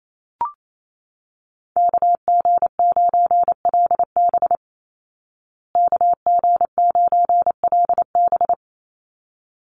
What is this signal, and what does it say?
Morse code, heard as a keyed tone, about 28 wpm, 700 Hz. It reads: KG9LB KG9LB
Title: Call Signs Repeated 3x - 28wpm